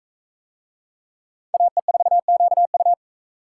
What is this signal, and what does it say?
Morse code, heard as a keyed tone, about 42 wpm, 700 Hz. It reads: AE4QU